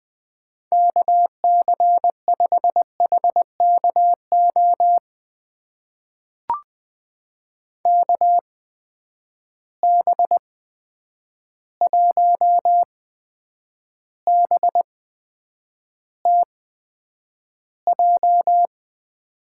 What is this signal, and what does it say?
Morse code, heard as a keyed tone, about 20 wpm, spaced as for 9 wpm, 700 Hz. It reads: KC5HKO KB1BTJ
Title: USA Call Signs - 6 Characters - 8x Character Spacing - 20wpm